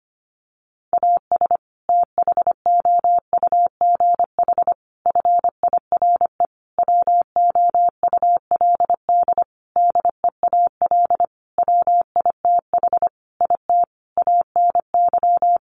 Morse code, heard as a keyed tone, about 25 wpm, 700 Hz. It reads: AS THOUGH FIRE WOULD DEAL WITH IT ANY
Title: Sentences from Top 1500 Words 25wpm